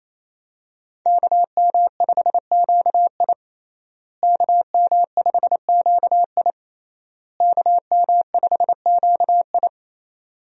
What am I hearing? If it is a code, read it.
Morse code, heard as a keyed tone, about 27 wpm, 700 Hz. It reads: KM5QI KM5QI KM5QI
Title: USA Call Signs - 5 Characters - Repeated 3x - 28wpm